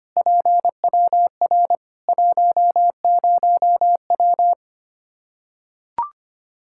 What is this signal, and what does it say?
Morse code, heard as a keyed tone, about 25 wpm, 700 Hz. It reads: PWR 10W
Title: Practice QSO - Exchange 2 - Info - 25wpm